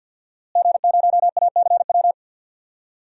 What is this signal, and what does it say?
Morse code, heard as a keyed tone, about 51 wpm, 700 Hz. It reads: G0ACW
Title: DXCC-100 Call Signs - 5 Characters - Repeated 3x - 50wpm